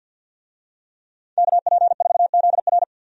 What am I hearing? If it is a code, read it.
Morse code, heard as a keyed tone, about 52 wpm, 700 Hz. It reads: KP4ZR
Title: USA Call Signs - 5 Characters - 50wpm